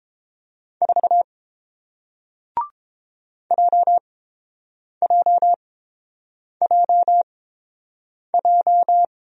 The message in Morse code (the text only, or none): 4 JJJJ